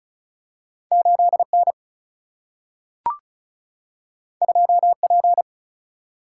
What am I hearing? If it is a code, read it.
Morse code, heard as a keyed tone, about 36 wpm, 700 Hz. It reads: 8 N 2 P
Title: Single-Letter Single-Number 35wpm